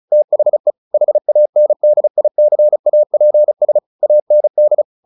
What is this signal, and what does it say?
Morse code, heard as a keyed tone, about 34 wpm, 600 Hz. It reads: THE HANDICAPS AND
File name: the old man and the sea 2 35wpm